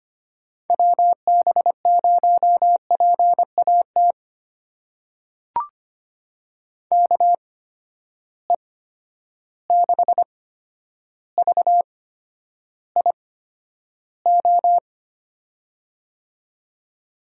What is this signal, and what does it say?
Morse code, heard as a keyed tone, about 25 wpm, spaced as for 11 wpm, 700 Hz. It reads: WB0PAT KE6VIO
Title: USA Call Signs - 8x Character Spacing - 25wpm